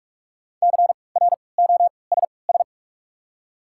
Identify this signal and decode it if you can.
Morse code, heard as a keyed tone, about 43 wpm, spaced as for 25 wpm, 700 Hz. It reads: CROSS
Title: 3x Character Spacing - Top 500 Words - 45wpm